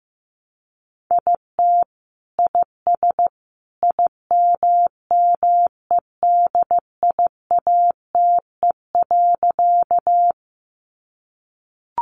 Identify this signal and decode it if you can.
Morse code, heard as a keyed tone, about 15 wpm, 700 Hz. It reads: IT IS IMMEDIATE.
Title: Sentences from Top 1200 Words 15wpm